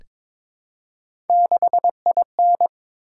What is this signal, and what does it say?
Morse code, heard as a keyed tone, about 22 wpm, 700 Hz. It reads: 6IN
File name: Three Letter-Number Combinations - 22wpm